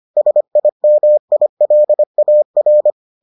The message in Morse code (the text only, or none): SIMILAR